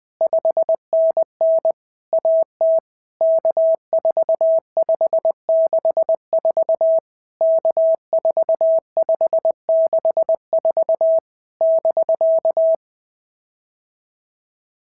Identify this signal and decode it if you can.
Morse code, heard as a keyed tone, about 20 wpm, 650 Hz. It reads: 5NN AT K4564 K4564 <BK>
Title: POTA QSO Practice - 20wpm